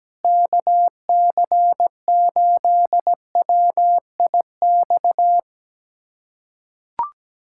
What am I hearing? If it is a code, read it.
Morse code, heard as a keyed tone, about 17 wpm, 700 Hz. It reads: KC8WIX
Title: USA Callsign DE Callsign 17wpm